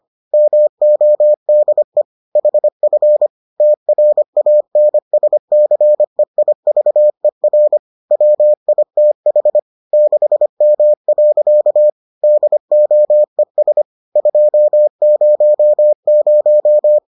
Morse code, heard as a keyed tone, about 25 wpm, 600 Hz. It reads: MODE HF TRANSCEIVER WITH 6M. DOES 200